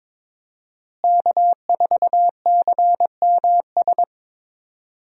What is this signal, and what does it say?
Morse code, heard as a keyed tone, about 22 wpm, 700 Hz. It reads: K4CMS
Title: USA Call Signs - 5 Characters - Repeated 2x - 22wpm